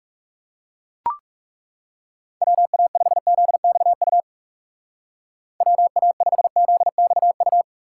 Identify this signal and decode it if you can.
Morse code, heard as a keyed tone, about 36 wpm, 700 Hz. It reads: WA5ZXU WA5ZXU
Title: Speed Racing - USA 2x3 Call Signs 30wpm